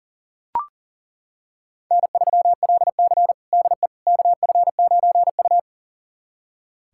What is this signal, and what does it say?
Morse code, heard as a keyed tone, about 40 wpm, 700 Hz. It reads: N3LC DE KF9U
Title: USA Callsign DE Callsign - 4 Characters - 40wpm